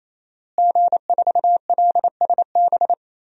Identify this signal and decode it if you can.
Morse code, heard as a keyed tone, about 28 wpm, 700 Hz. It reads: G4LSB
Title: DXCC-100 Call Signs - 5 Characters - Repeated 3x - 28wpm